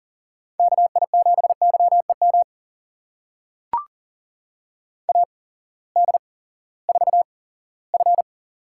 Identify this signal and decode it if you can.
Morse code, heard as a keyed tone, about 38 wpm, spaced as for 20 wpm, 700 Hz. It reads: KI7YEM AD4F
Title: USA Call Signs - 8x Character Spacing - 40wpm